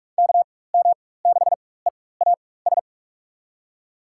Morse code, heard as a keyed tone, about 44 wpm, spaced as for 21 wpm, 700 Hz. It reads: KM6EAS
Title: USA Call Signs - 6 Characters - 4x Character Spacing - 45wpm